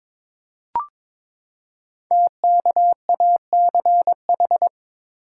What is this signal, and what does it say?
Morse code, heard as a keyed tone, about 22 wpm, 700 Hz. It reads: TKACH
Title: Code Groups - 5 Letters - 22wpm